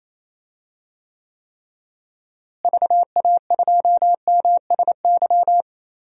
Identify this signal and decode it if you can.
Morse code, heard as a keyed tone, about 27 wpm, 700 Hz. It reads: VA2MSY
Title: DXCC-100 Call Signs - 6 Characters - Repeated 3x - 28wpm